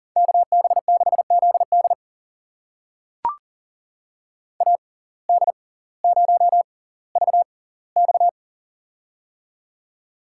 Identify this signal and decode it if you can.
Morse code, heard as a keyed tone, about 40 wpm, spaced as for 20 wpm, 700 Hz. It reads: KB6ZD AD0VX
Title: USA Call Signs - 5 Characters - 6x Character Spacing - 40wpm